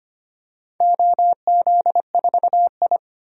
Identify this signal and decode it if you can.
Morse code, heard as a keyed tone, about 25 wpm, 700 Hz. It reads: OZ4I
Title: DXCC-100 Call Signs - 4 Characters - 25wpm